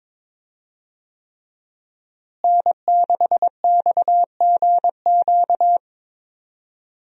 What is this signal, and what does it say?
Morse code, heard as a keyed tone, about 22 wpm, 700 Hz. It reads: N6XGQ
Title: USA Call Signs - 5 Characters - Repeated 3x - 22wpm